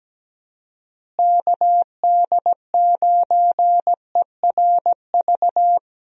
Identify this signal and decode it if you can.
Morse code, heard as a keyed tone, about 17 wpm, 700 Hz. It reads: KD9ERV
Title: USA Call Signs - 6 Characters - Repeated 3x - 17wpm